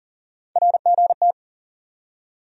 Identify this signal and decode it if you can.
Morse code, heard as a keyed tone, about 40 wpm, 700 Hz. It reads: RGT
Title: USA Call Sign Postfixes - 3 Letters 40wpm